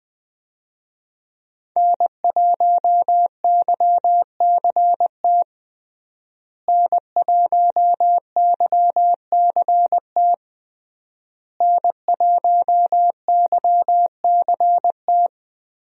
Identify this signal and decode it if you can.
Morse code, heard as a keyed tone, about 20 wpm, 700 Hz. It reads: N1YCT N1YCT N1YCT
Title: USA Call Signs - 5 Characters - Repeated 3x - 20wpm